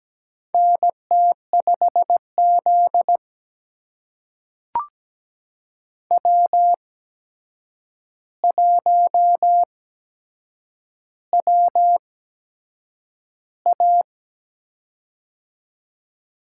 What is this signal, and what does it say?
Morse code, heard as a keyed tone, about 17 wpm, spaced as for 7 wpm, 700 Hz. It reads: NT5Z W1WA
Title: USA Call Signs - 4 Characters - 8x Character Spacing - 17wpm